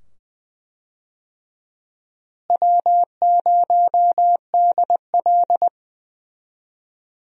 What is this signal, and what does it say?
Morse code, heard as a keyed tone, about 21 wpm, 700 Hz. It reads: W0DL
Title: USA Call Signs - 4 Characters - Repeated 2x - 20wpm